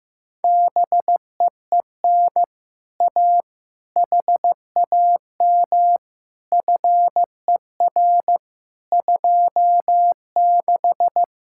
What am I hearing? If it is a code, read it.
Morse code, heard as a keyed tone, about 15 wpm, 700 Hz. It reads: BEEN A HAM FER 26
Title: Speed-Racing Common QSO Phrases Encore - 15wpm